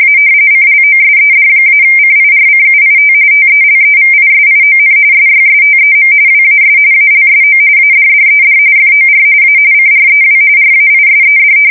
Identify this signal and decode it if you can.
RTTY, about 45.45 baud. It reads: A CLOSED-CIRCUIT TELEVISION SYSTEM WAS ALSO ADDED FOR A CREW RAISED
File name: fsk170 / battleshipnj